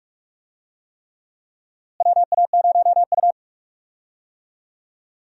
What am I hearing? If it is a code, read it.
Morse code, heard as a keyed tone, about 46 wpm, 700 Hz. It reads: WA0U
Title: USA Call Signs - 4 Characters - Repeated 3x - 45wpm